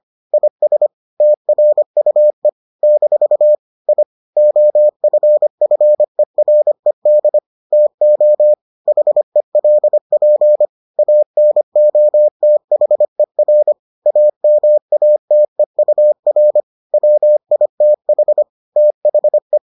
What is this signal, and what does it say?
Morse code, heard as a keyed tone, about 25 wpm, 600 Hz. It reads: IS TRUE - I OFFERED TO HELP ANOTHER AMATEUR WITH THE